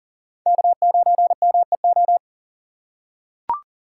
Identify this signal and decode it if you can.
Morse code, heard as a keyed tone, about 40 wpm, 700 Hz. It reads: K9MEO
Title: USA Call Signs - 3x Character Spacing - 40wpm